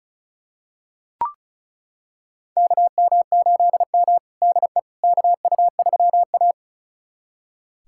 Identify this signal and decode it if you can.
Morse code, heard as a keyed tone, about 36 wpm, 700 Hz. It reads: KM8M DE KU3A
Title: USA Callsign DE Callsign - 4 Characters - 35wpm